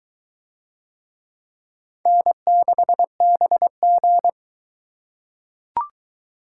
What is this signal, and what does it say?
Morse code, heard as a keyed tone, about 22 wpm, 700 Hz. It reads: N6BG